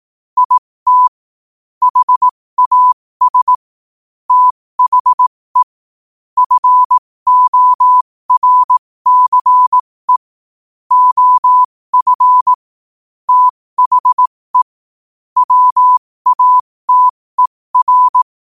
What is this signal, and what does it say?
Morse code, heard as a keyed tone, about 18 wpm, spaced as for 15 wpm, 1000 Hz. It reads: IT HAS THE FORCE OF THE WATER
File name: qotdfiles / 15wpm